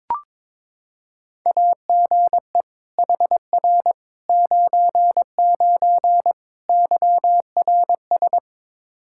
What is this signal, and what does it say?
Morse code, heard as a keyed tone, about 22 wpm, 700 Hz. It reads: AGE HR 99 YRS
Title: Practice QSO - Exchange 3 - More Info - 22wpm